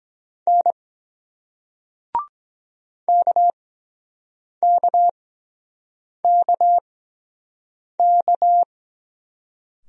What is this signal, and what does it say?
Morse code, heard as a keyed tone, about 23 wpm, spaced as for 7 wpm, 700 Hz. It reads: N KKKK